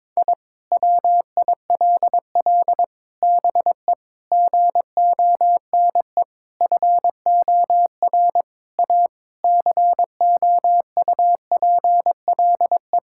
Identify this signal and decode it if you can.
Morse code, heard as a keyed tone, about 22 wpm, 700 Hz. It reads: I WILL BE GONE FOR A COUPLE
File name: practice / 22wpm / Sentences with 4-Word Phrases - 22wpm